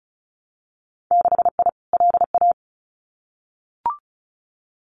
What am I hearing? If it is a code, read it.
Morse code, heard as a keyed tone, about 34 wpm, 700 Hz. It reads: 6I LA
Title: Winter Field Day Exchanges Encore 35wpm